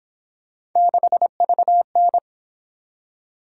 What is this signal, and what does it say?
Morse code, heard as a keyed tone, about 28 wpm, 700 Hz. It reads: 6VN